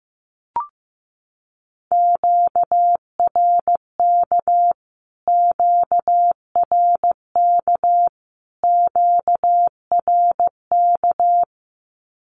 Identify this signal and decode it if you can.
Morse code, heard as a keyed tone, about 15 wpm, 700 Hz. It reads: QRK QRK QRK
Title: Q-Codes Repeated 3x 15wpm